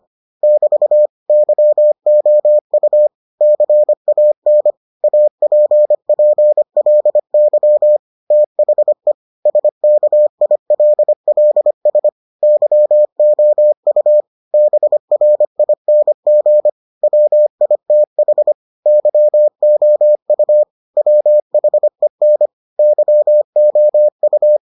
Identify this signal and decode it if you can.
Morse code, heard as a keyed tone, about 25 wpm, 600 Hz. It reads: <BT> YOU CAN APPLY THE SKILLS YOU BRING WITH YOU WHEN YOU